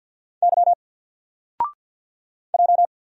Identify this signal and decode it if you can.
Morse code, heard as a keyed tone, about 47 wpm, 700 Hz. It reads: Y J
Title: Single Letter-Number Alt 50wpm